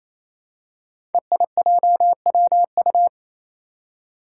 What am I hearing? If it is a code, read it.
Morse code, heard as a keyed tone, about 28 wpm, 700 Hz. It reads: EIJWU
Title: Five Challenging Letter-Number Combinations 28wpm